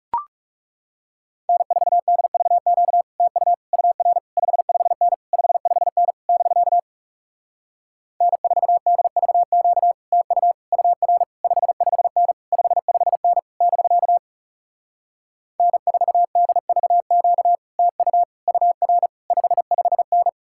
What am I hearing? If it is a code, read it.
Morse code, heard as a keyed tone, about 40 wpm, 700 Hz. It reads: N4DVQ TU UR 55N 55N <BK> N4DVQ TU UR 55N 55N <BK> N4DVQ TU UR 55N